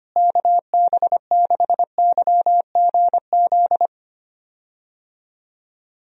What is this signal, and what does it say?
Morse code, heard as a keyed tone, about 25 wpm, 700 Hz. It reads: KB6YGZ
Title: USA Call Signs - 6 Characters - Repeated 2x - 25wpm